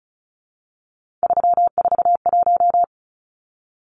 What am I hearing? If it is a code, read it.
Morse code, heard as a keyed tone, about 34 wpm, 700 Hz. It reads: 341
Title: Numbers - 3-Digit 35wpm